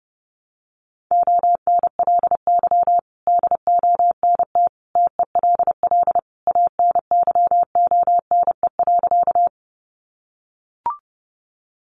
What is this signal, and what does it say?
Morse code, heard as a keyed tone, about 28 wpm, 700 Hz. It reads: ONLY DONT TELL ANYONE.
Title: Sentences from Top 900 Words 30wpm